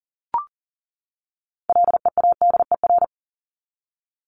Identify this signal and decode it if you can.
Morse code, heard as a keyed tone, about 40 wpm, 700 Hz. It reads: LEADER
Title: Top 1501-1750 Words 40wpm